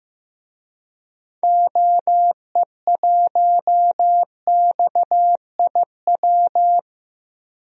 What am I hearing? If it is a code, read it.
Morse code, heard as a keyed tone, about 15 wpm, 700 Hz. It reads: OE1XIW
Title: DXCC-100 Call Signs - 6 Characters - 15wpm